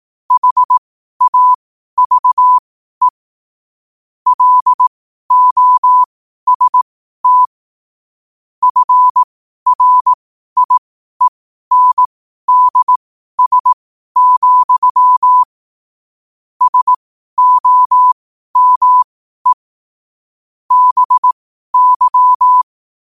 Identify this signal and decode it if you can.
Morse code, heard as a keyed tone, about 18 wpm, spaced as for 12 wpm, 1000 Hz. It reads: HAVE LOST FRIENDS, SOME BY
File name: qotdfiles / 12wpm